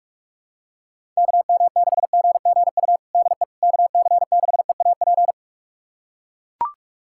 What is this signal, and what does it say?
Morse code, heard as a keyed tone, about 42 wpm, 700 Hz. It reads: KM6GGU DE KC6EAP
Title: USA Callsign DE Callsign 45wpm